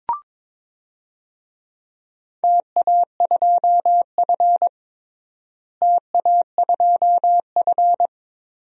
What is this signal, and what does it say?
Morse code, heard as a keyed tone, about 22 wpm, 700 Hz. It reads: TA2F TA2F
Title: DXCC-100 Call Signs - 4 Characters - Repeated 2x - 22wpm